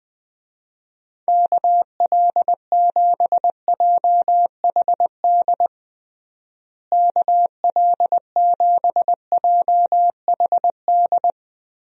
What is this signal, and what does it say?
Morse code, heard as a keyed tone, about 21 wpm, 700 Hz. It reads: KL7JHD KL7JHD
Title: USA Call Signs - 6 Characters - Repeated 3x - 20wpm